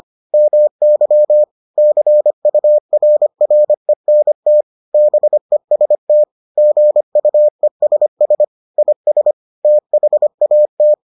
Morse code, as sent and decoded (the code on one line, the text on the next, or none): -- -.-- / -.-. ..- .-. .-. -. - / -... ... - / --. ..- . ... ... / .. ... / - .... .- -
MY CURRENT BEST GUESS IS THAT